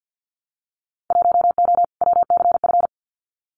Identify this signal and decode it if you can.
Morse code, heard as a keyed tone, about 51 wpm, 700 Hz. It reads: 1O WCF
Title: Winter Field Day Exchanges 50wpm